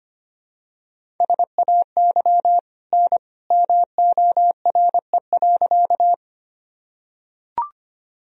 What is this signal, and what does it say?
Morse code, heard as a keyed tone, about 25 wpm, 700 Hz. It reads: SAY N MORE.
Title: Sentences from Top 200 Words and Abbreviations 25wpm